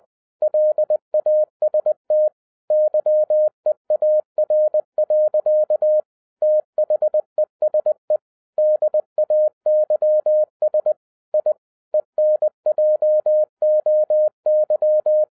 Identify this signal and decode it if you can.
Morse code, heard as a keyed tone, about 20 wpm, 610 Hz. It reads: LAST YEAR. THESE DAYS I ENJOY
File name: 2012-01-mp3 / n05-CAME BACK TO HAM RADIO AFTER 30 YEARS OF INTERVAL-20wpm